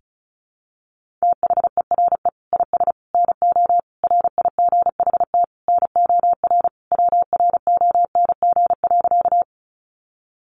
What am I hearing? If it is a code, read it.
Morse code, heard as a keyed tone, about 34 wpm, 700 Hz. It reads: THERE IS NO RIGHT NOR WRONG.
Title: Sentences from Top 1500 Words 35wpm